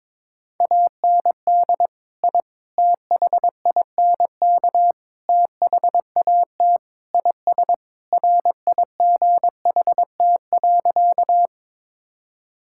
Morse code, heard as a keyed tone, about 22 wpm, 700 Hz. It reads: AND I THINK THAT IS RIGHT.
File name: practice / 22wpm / Sentences with 4-Word Phrases - 22wpm